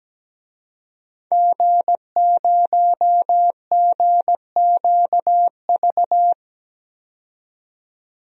Morse code, heard as a keyed tone, about 17 wpm, 700 Hz. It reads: G0GQV
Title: DXCC-100 Call Signs - 5 Characters - 17wpm